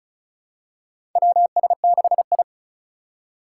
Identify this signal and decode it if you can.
Morse code, heard as a keyed tone, about 34 wpm, 700 Hz. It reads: WS6I
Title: USA Call Signs - 4 Characters - 4x Character Spacing - 35wpm